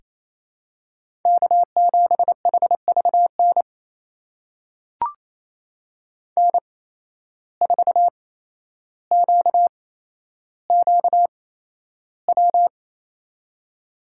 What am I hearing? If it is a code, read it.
Morse code, heard as a keyed tone, about 28 wpm, spaced as for 12 wpm, 700 Hz. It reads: K7HVN N4QQW
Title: USA Call Signs - 5 Characters - 8x Character Spacing - 28wpm